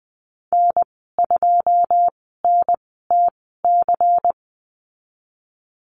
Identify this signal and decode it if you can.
Morse code, heard as a keyed tone, about 20 wpm, spaced as for 15 wpm, 700 Hz. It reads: N2NTC
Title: 122 Lesson 20g - Number 3 - Call Signs 20wpm